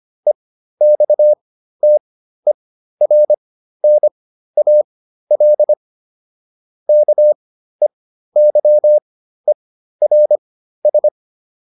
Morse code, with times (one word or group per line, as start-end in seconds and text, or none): 0.3-5.7 EXTERNAL
6.9-11.1 KEYERS